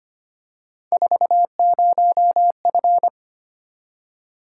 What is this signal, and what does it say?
Morse code, heard as a keyed tone, about 25 wpm, 700 Hz. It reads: 40F
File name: Three Letter-Number Combinations - 25wpm